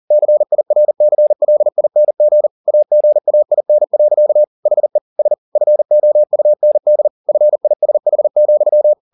Morse code, heard as a keyed tone, about 40 wpm, 600 Hz. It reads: CIRCLING AGAIN. HE S FOUND FISH,